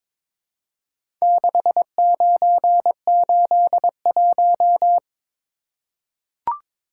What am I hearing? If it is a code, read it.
Morse code, heard as a keyed tone, about 23 wpm, 700 Hz. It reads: 6981
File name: practice / 22wpm / Four Letter-Number Combinations - 22wpm